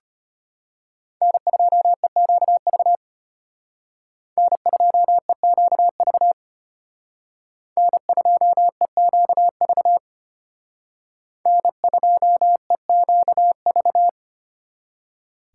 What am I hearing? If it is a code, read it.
Morse code, heard as a keyed tone, about 31 wpm, 700 Hz. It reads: N2EQV N2EQV N2EQV N2EQV
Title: Speed Racing - USA 1x3 Call Signs 25wpm